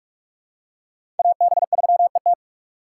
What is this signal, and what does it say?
Morse code, heard as a keyed tone, about 42 wpm, 700 Hz. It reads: AB3ET
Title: USA Call Signs - 4x Character Spacing - 45wpm